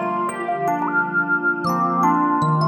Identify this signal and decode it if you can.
There is no code.